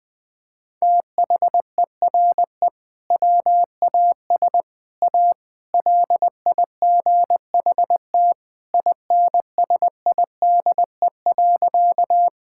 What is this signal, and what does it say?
Morse code, heard as a keyed tone, about 20 wpm, 700 Hz. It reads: THERE WAS A LIGHT INSIDE.
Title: Sentences from Top 1000 Words - Review Part 6 - 20wpm